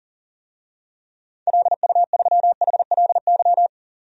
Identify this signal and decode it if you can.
Morse code, heard as a keyed tone, about 40 wpm, 700 Hz. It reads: LU3HLY
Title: DXCC-100 Call Signs - 6 Characters - Repeated 3x - 40wpm